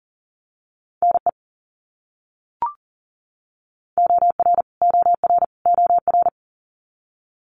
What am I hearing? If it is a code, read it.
Morse code, heard as a keyed tone, about 40 wpm, 700 Hz. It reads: NE OR OR OR
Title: US States Repeated 3x 40wpm